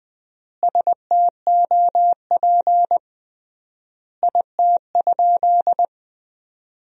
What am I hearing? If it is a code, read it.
Morse code, heard as a keyed tone, about 20 wpm, 700 Hz. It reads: STOP IT?
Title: 3x Word-Spacing - Sentences from Top 300 Words - 20wpm